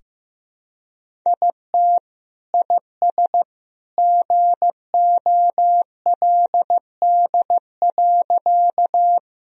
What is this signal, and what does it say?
Morse code, heard as a keyed tone, about 15 wpm, 700 Hz. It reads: IT IS GOLD.
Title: Sentences from Top 1000 Words - Review Part 3 - 15wpm